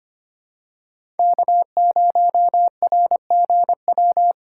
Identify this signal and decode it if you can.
Morse code, heard as a keyed tone, about 25 wpm, 700 Hz. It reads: K0RGW
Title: USA Call Signs - 3x Character Spacing - 25wpm